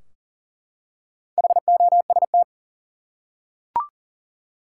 Morse code, heard as a keyed tone, about 40 wpm, 700 Hz. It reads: HOST
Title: Four Letter Words - 40wpm